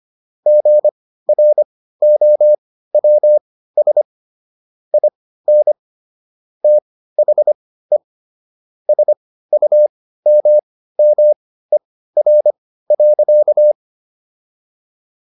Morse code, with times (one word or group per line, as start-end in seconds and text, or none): 0.5-4.0 GROWS
4.9-5.7 IN
6.6-8.0 THE
8.9-13.7 SUMMER.